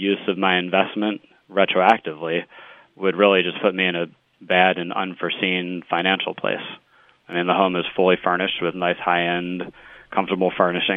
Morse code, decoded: none

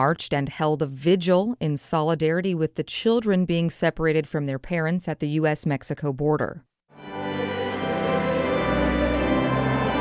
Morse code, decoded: none

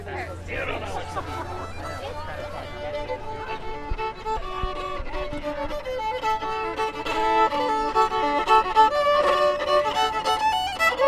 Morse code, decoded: none

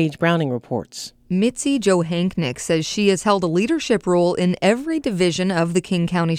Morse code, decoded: none

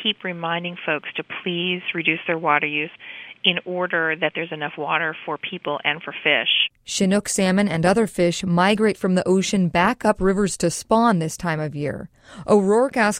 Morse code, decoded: none

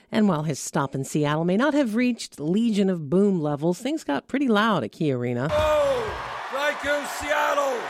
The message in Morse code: none

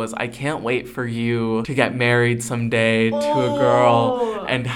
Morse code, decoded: none

